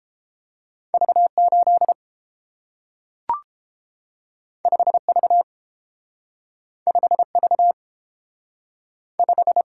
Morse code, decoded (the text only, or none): V 8 5 V 5 V 5